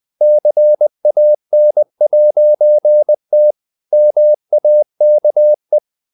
Morse code, heard as a keyed tone, about 20 wpm, 600 Hz. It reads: CAN'T MAKE
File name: CW / Practice / EnglishSayings20